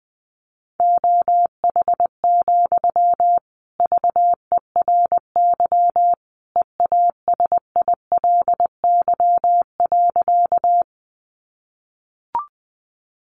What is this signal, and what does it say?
Morse code, heard as a keyed tone, about 20 wpm, 700 Hz. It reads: OH, VERY EASILY.
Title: Sentences from Top 900 Words 20wpm